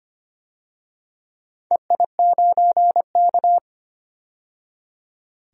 Morse code, heard as a keyed tone, about 25 wpm, 700 Hz. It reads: EI9K
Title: DXCC-100 Call Signs - 4 Characters - 25wpm